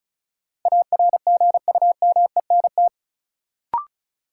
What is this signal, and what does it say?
Morse code, heard as a keyed tone, about 36 wpm, 700 Hz. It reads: ARGUMENT
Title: Sentences from Top 1400 Words - New Word Drill - 35wpm